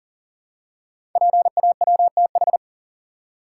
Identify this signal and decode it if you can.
Morse code, heard as a keyed tone, about 38 wpm, 690 Hz. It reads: PAWTH
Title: Code Groups - 5 Letters - 40wpm